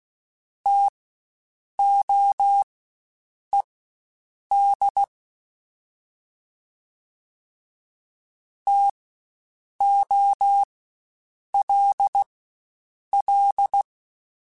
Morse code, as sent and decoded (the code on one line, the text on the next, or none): - --- . -.. / - --- .-.. .-..
TOED TOLL